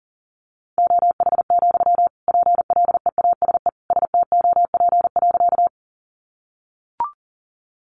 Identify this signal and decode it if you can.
Morse code, heard as a keyed tone, about 40 wpm, 700 Hz. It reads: OH, PLEASE STOP.
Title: Sentences from Top 1500 Words 40wpm